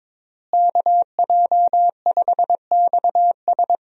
Morse code, heard as a keyed tone, about 22 wpm, 700 Hz. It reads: KJ5XS
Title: USA Call Signs - 5 Characters - Repeated 2x - 22wpm